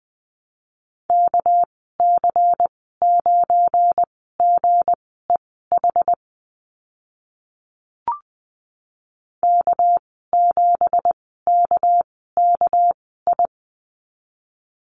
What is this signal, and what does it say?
Morse code, heard as a keyed tone, about 20 wpm, spaced as for 14 wpm, 700 Hz. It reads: KC9GEH K7KKI
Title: 213 Lesson 34g - Number 8 - Call Signs 20wpm